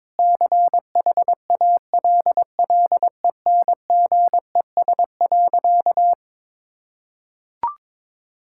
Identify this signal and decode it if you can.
Morse code, heard as a keyed tone, about 22 wpm, 700 Hz. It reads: CHALLENGES.
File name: Difficult Sentences with Hints - 22wpm